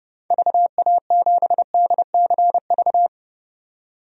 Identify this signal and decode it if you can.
Morse code, heard as a keyed tone, about 30 wpm, 700 Hz. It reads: VA7DCV
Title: DXCC-100 Call Signs - 6 Characters - 30wpm